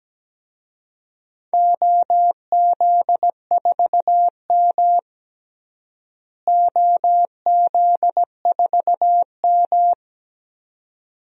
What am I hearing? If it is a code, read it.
Morse code, heard as a keyed tone, about 17 wpm, 700 Hz. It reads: OZ4M OZ4M
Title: DXCC-100 Call Signs - 4 Characters - Repeated 2x - 17wpm